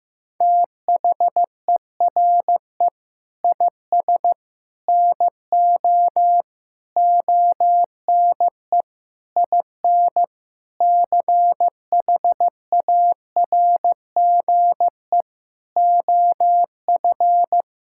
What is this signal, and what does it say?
Morse code, heard as a keyed tone, about 15 wpm, 700 Hz. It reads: THERE IS NO ONE IN CHARGE OF
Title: Sentences from Top 1000 Words - Review Part 6 - 15wpm